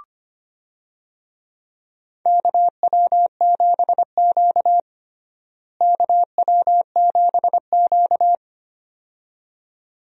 Morse code, heard as a keyed tone, about 25 wpm, 700 Hz. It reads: KW7Q KW7Q
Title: USA Call Signs - 4 Characters - Repeated 2x - 25wpm